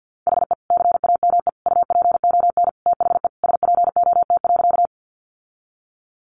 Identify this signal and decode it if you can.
Morse code, heard as a keyed tone, about 50 wpm, 700 Hz. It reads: HE CAME UPON THE SPOT.
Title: Sentences from Top 1100 Words 50wpm